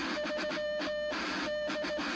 Morse code, heard as a keyed tone, about 17 wpm, 600 Hz, with noise. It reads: 3 D